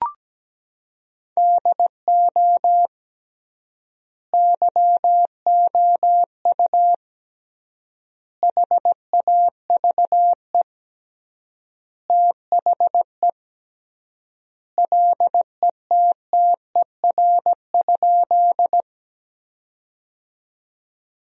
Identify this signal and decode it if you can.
Morse code, heard as a keyed tone, about 17 wpm, 700 Hz. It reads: DO YOU HAVE THE LETTER?
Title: 3x Word-Spacing - Sentences from Top 300 Words - 17wpm